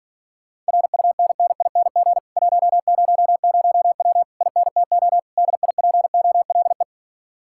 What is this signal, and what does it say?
Morse code, heard as a keyed tone, about 46 wpm, 700 Hz. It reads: RUNNING 100W INTO DIPOLE